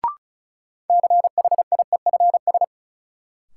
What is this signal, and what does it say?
Morse code, heard as a keyed tone, about 36 wpm, 700 Hz. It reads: CHIEFS
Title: Sets of 5 Words Encore - New Word Drill - 35wpm